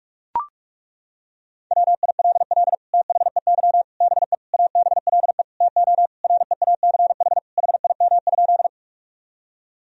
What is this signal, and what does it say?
Morse code, heard as a keyed tone, about 41 wpm, 700 Hz. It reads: WILL THEY BE ABLE TO REACH HIM?
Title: Sentences from Top 1000 Words - Review Part 2 - 45wpm